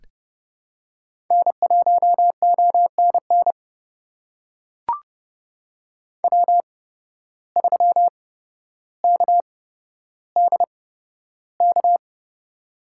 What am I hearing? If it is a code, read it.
Morse code, heard as a keyed tone, about 30 wpm, spaced as for 13 wpm, 700 Hz. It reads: N1ONN W3KDK
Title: USA Call Signs - 5 Characters - 8x Character Spacing - 30wpm